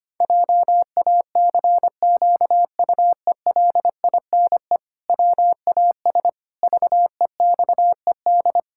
Morse code, heard as a keyed tone, about 25 wpm, 700 Hz. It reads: JACQUELINE WAS VEXED
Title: Pangram Challenge v2 - 25wpm